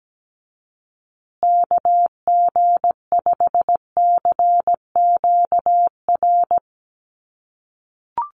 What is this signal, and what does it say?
Morse code, heard as a keyed tone, about 17 wpm, 700 Hz. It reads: KG5CQR
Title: USA Call Signs Repeated 3x 17wpm